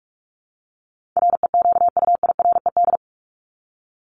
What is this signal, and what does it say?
Morse code, heard as a keyed tone, about 44 wpm, 700 Hz. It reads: REQUIRED
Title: ED Suffix - Top 100 Words with Base Words 45wpm